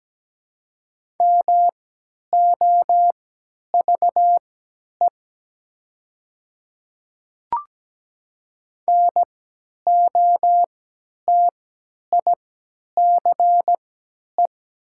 Morse code, heard as a keyed tone, about 17 wpm, spaced as for 10 wpm, 700 Hz. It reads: MOVE NOTICE
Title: 3x Character Spacing - Top 500 Words - 17wpm